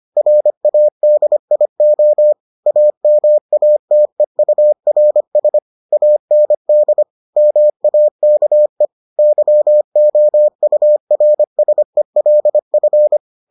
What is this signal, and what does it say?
Morse code, heard as a keyed tone, about 25 wpm, 600 Hz. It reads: RADIO AMATEURS AND MAKE YOURSELF